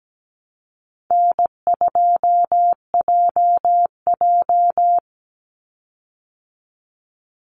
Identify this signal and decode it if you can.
Morse code, heard as a keyed tone, about 17 wpm, 700 Hz. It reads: N2JJ
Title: Easy Call Signs Repeated 2x - 17wpm